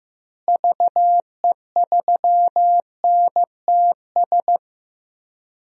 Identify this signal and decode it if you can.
Morse code, heard as a keyed tone, about 15 wpm, 700 Hz. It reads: VE3NTS
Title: DXCC-100 Call Signs - 6 Characters - Repeated 3x - 15wpm